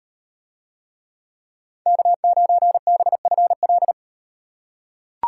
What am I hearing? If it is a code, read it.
Morse code, heard as a keyed tone, about 38 wpm, 700 Hz. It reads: K9BFL